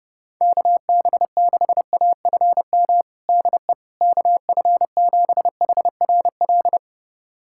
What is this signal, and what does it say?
Morse code, heard as a keyed tone, about 29 wpm, 700 Hz. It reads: KB6AFM DE KF7HRL
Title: USA Callsign DE Callsign - 6 Characters - 30wpm